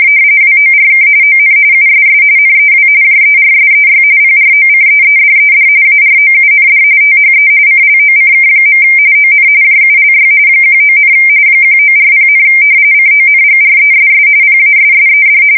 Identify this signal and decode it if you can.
RTTY, about 45.45 baud. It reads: WORK ON USS NEW JERSEY PROCEEDED AROUND THE CLOCK. ON JULY 19, 1940, CONGRESS PASSED
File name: fsk170 / battleshipnj